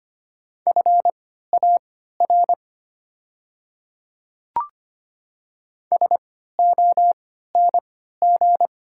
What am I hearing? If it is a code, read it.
Morse code, heard as a keyed tone, about 25 wpm, spaced as for 14 wpm, 700 Hz. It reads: FAR SONG
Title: 3x Character Spacing - Top 500 Words - 25wpm